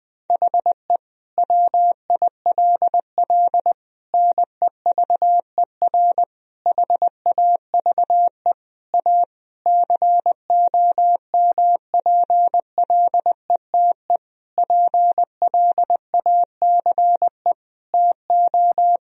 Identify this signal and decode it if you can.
Morse code, heard as a keyed tone, about 21 wpm, 700 Hz. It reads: HE WILL NEVER HAVE A COMPLETE PLACE TO
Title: Sentences from Top 1000 Words - Review Part 2 - 20wpm